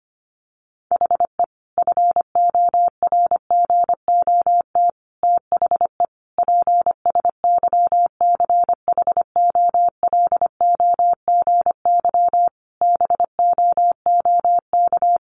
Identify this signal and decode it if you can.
Morse code, heard as a keyed tone, about 25 wpm, 700 Hz. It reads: HE FORGOT THE PSYCHOLOGY BOOK
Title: Sentences from Top 1400 Words 25wpm